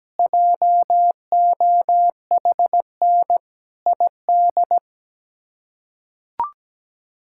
Morse code, heard as a keyed tone, about 17 wpm, 700 Hz. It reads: JOHN ID
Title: K1USN SST QSO Practice - 17wpm